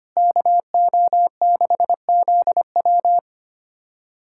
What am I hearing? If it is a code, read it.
Morse code, heard as a keyed tone, about 25 wpm, 700 Hz. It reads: KO6ZW
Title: USA Callsign DE Callsign - 5 Characters - 25wpm